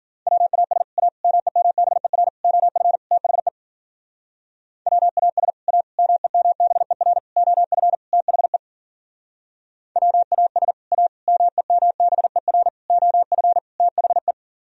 Morse code, heard as a keyed tone, about 48 wpm, 700 Hz. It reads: WAS A MEMBER OF THE WAS A MEMBER OF THE WAS A MEMBER OF THE
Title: Speed-Racing - Sets of 5 Words 40wpm